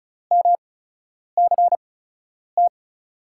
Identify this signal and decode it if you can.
Morse code, heard as a keyed tone, about 36 wpm, spaced as for 10 wpm, 700 Hz. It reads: MCT
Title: USA Call Signs - 6 Characters - 8x Character Spacing - 35wpm